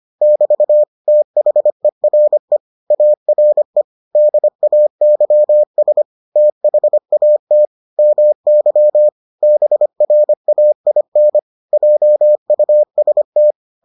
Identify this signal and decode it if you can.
Morse code, heard as a keyed tone, about 25 wpm, 600 Hz. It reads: <BT> THERE ARE DAYS THAT MY BRAIN JUST